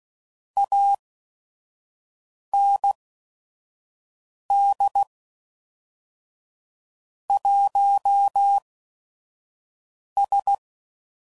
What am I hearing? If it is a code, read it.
Morse code, heard as a keyed tone, about 16 wpm, spaced as for 5 wpm, 800 Hz. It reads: AND 1S
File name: CW Code Groups @5WPM / Technical Subjects / Gates Intro